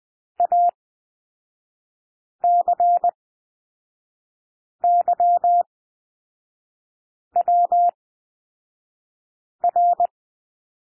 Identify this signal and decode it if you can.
Morse code, heard as a keyed tone, about 20 wpm, spaced as for 5 wpm, 700 Hz. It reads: ACYWR
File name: SOUND LETTERS 20 wpm